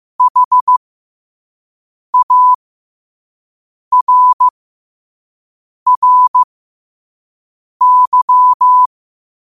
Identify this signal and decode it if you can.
Morse code, heard as a keyed tone, about 15 wpm, spaced as for 5 wpm, 1000 Hz. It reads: HARRY
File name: qotdfiles / 5wpm